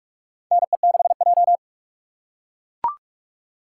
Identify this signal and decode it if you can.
Morse code, heard as a keyed tone, about 42 wpm, 700 Hz. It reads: NE6J
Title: USA Call Signs - 4x Character Spacing - 45wpm